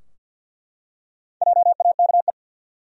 Morse code, heard as a keyed tone, about 49 wpm, 700 Hz. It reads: JAKE
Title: USA Common Names - 50wpm